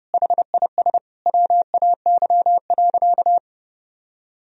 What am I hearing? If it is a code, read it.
Morse code, heard as a keyed tone, about 30 wpm, 700 Hz. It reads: HIS WAY.
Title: Sentences from Top 300 Words and Abbreviations 30wpm